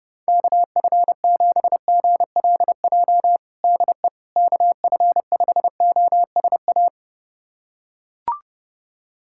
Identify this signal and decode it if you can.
Morse code, heard as a keyed tone, about 31 wpm, 700 Hz. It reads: KF7GLJ DE KF5OSA
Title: USA Callsign DE Callsign - 6 Characters - 30wpm